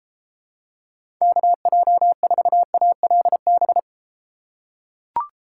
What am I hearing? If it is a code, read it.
Morse code, heard as a keyed tone, about 28 wpm, 700 Hz. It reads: KJ4ALB